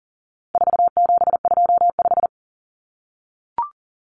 Morse code, heard as a keyed tone, about 40 wpm, 700 Hz. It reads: 4725
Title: Numbers - 4-Digit 40wpm